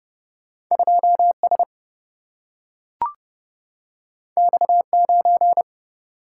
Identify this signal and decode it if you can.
Morse code, heard as a keyed tone, about 30 wpm, 700 Hz. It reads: 2 S X 9
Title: Single-Letter Single-Number 30wpm